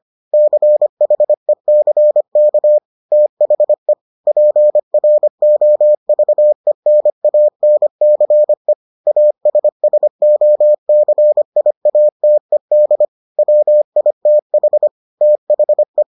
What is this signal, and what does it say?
Morse code, heard as a keyed tone, about 25 wpm, 600 Hz. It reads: CHECK THE PROVENANCE ASSOCIATED WITH THE